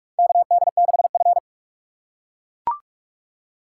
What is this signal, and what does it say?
Morse code, heard as a keyed tone, about 44 wpm, 700 Hz. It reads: KD6F
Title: USA Call Signs - 4 Characters - 3x Character Spacing - 45wpm